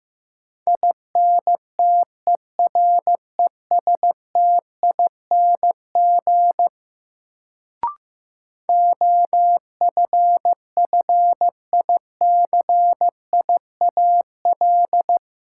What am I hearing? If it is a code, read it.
Morse code, heard as a keyed tone, about 15 wpm, 700 Hz. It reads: INTERESTING OFFICIAL
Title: Sentences from Top 1000 Words - New Word Drill - 15wpm